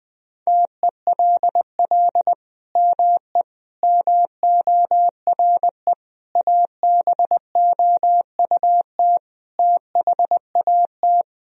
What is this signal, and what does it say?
Morse code, heard as a keyed tone, about 20 wpm, 700 Hz. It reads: TELL ME MORE ABOUT THAT
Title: Sentences from Top 1000 Words - Review Part 4 - 20wpm